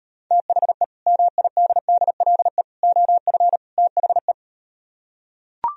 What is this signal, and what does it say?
Morse code, heard as a keyed tone, about 38 wpm, 700 Hz. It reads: THE MIDDLE OF THE